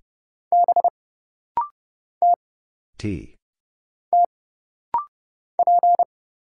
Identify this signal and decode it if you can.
Morse code, heard as a keyed tone, about 30 wpm, 700 Hz. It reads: B T T P
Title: Single Letter-Number - Rapid-Fire 30wpm